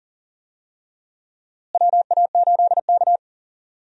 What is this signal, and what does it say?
Morse code, heard as a keyed tone, about 39 wpm, 700 Hz. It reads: WA8K
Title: USA Call Signs - 4 Characters - Repeated 2x - 40wpm